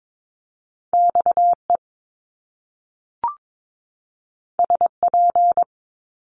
Morse code, heard as a keyed tone, about 22 wpm, 700 Hz. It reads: XE SP
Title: Top Two Letter Combinations in Words 22wpm